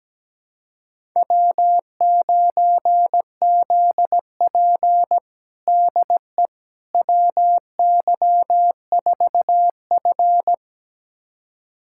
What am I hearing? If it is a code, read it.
Morse code, heard as a keyed tone, about 17 wpm, 700 Hz. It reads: W9ZP DE WY4F
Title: USA Callsign DE Callsign - 4 Characters - 17wpm